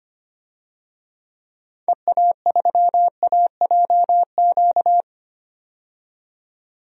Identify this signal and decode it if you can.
Morse code, heard as a keyed tone, about 25 wpm, 700 Hz. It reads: EA3AJQ
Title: DXCC-100 Call Signs - 6 Characters - 25wpm